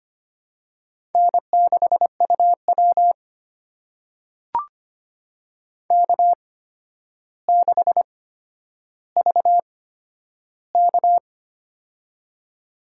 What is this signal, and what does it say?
Morse code, heard as a keyed tone, about 25 wpm, spaced as for 11 wpm, 700 Hz. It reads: N6UW K6VK
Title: USA Call Signs - 4 Characters - 8x Character Spacing - 25wpm